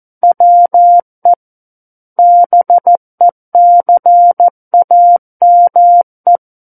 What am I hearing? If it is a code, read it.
Morse code, heard as a keyed tone, about 14 wpm, 700 Hz. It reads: WE BECAME
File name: Book One / 05-Chapter 5 - 14 WPM